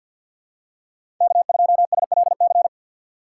Morse code, heard as a keyed tone, about 53 wpm, 700 Hz. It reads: K2SLC
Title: USA Call Signs - 5 Characters - 2x Character Spacing - 50wpm